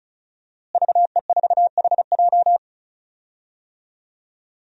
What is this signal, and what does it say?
Morse code, heard as a keyed tone, about 34 wpm, 700 Hz. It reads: VE4HJ
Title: DXCC-100 Call Signs - 5 Characters - Repeated 2x - 35wpm